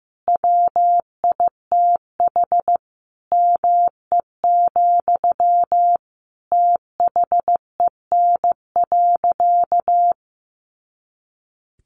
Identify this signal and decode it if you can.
Morse code, heard as a keyed tone, about 15 wpm, 700 Hz. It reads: WITH ME, THEN.
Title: Sentences from Top 200 Words 15wpm